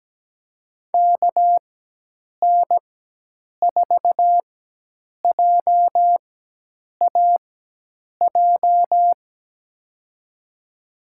Morse code, heard as a keyed tone, about 17 wpm, spaced as for 8 wpm, 700 Hz. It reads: KN4JAJ
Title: USA Call Signs - 6 Characters - 4x Character Spacing - 17wpm